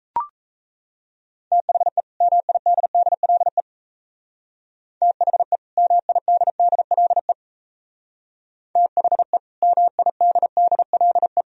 Morse code, heard as a keyed tone, about 34 wpm, 700 Hz. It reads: THE MIDDLE THE MIDDLE THE MIDDLE
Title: Speed-Racing - Sets of 2 Words 28wpm